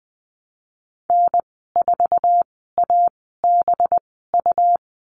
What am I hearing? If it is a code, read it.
Morse code, heard as a keyed tone, about 20 wpm, spaced as for 15 wpm, 700 Hz. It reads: N4ABU
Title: 193 Lesson 31g - Letter v - Call Signs 20wpm